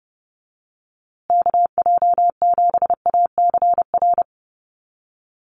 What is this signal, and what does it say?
Morse code, heard as a keyed tone, about 30 wpm, 700 Hz. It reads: KJ7ACR
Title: USA Call Signs Repeated 2x 30wpm